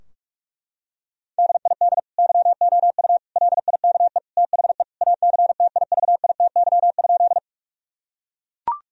Morse code, heard as a keyed tone, about 41 wpm, 700 Hz. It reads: DID YOU LIKE THE ACTIVITY?
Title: Sentences from Top 1000 Words - Review Part 4 - 45wpm